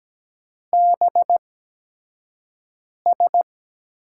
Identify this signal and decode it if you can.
Morse code, heard as a keyed tone, about 17 wpm, spaced as for 5 wpm, 700 Hz. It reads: BS
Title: USA Call Signs - 5 Characters - 8x Character Spacing - 17wpm